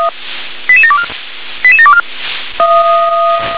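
Five-tone selective call (ZVEI), heard as quick runs of five tones, tones of about 80 ms. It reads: 88724 88724
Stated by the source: ZVEI five-tone selective call received on SDR.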